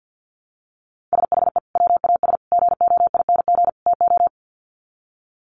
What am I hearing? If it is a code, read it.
Morse code, heard as a keyed tone, about 50 wpm, 700 Hz. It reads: SHE WAS GOING TO
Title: Sets of 4 Words 50wpm